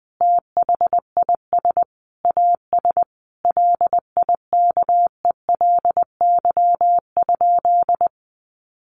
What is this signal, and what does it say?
Morse code, heard as a keyed tone, about 19 wpm, 700 Hz. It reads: THIS AS LIKELY?
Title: Sentences from Top 1100 Words 20wpm